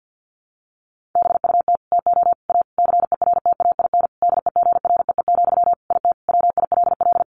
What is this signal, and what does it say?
Morse code, heard as a keyed tone, about 50 wpm, 700 Hz. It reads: BUT TO A CERTAIN DEGREE, IT WILL
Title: Sentences from Top 1500 Words 50wpm